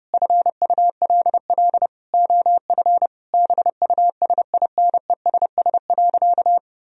Morse code, heard as a keyed tone, about 30 wpm, 700 Hz. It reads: FULL OF BUSINESS.
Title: Sentences from Top 1000 Words - Review Part 3 - 30wpm